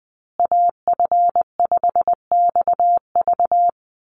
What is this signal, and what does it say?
Morse code, heard as a keyed tone, about 19 wpm, 700 Hz. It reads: AF5XV
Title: USA Call Signs Repeated 2x 20wpm